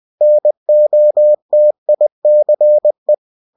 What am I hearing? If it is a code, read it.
Morse code, heard as a keyed tone, about 20 wpm, 600 Hz. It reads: NOTICE
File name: cw-books / winnie-the-pooh / winnie-20wpm-page-0119